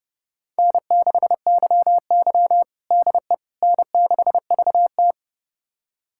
Morse code, heard as a keyed tone, about 30 wpm, 700 Hz. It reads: N6YY DE N6VT
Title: USA Callsign DE Callsign - 4 Characters - 30wpm